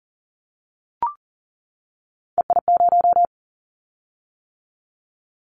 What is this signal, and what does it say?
Morse code, heard as a keyed tone, about 42 wpm, 700 Hz. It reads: EI0
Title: Callsign Prefixes for Top 50 DXCC - 40wpm